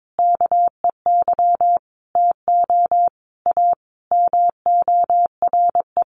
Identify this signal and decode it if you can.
Morse code, heard as a keyed tone, about 22 wpm, 700 Hz. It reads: KEY TO A MORE
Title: Sentences from Top 1400 Words 22wpm